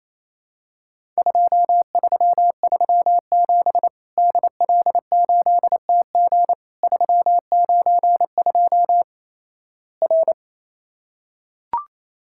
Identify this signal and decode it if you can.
Morse code, heard as a keyed tone, about 28 wpm, 700 Hz. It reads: 2337 DL8TG 392 R
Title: Worked All Europe DX Contest - QTC Exchange - 28wpm